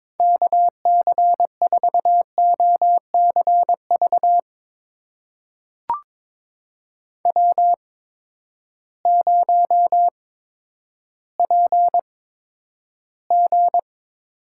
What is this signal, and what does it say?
Morse code, heard as a keyed tone, about 22 wpm, spaced as for 11 wpm, 700 Hz. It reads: KC4OCV W0PG
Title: USA Call Signs - 8x Character Spacing - 22wpm